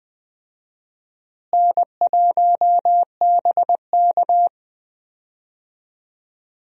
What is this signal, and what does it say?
Morse code, heard as a keyed tone, about 20 wpm, 700 Hz. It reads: N1BK
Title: USA Call Signs - 4 Characters - Repeated 2x - 20wpm